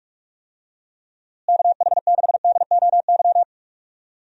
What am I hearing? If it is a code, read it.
Morse code, heard as a keyed tone, about 46 wpm, 700 Hz. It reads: KH6DOY
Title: DXCC-100 Call Signs - 6 Characters - Repeated 2x - 45wpm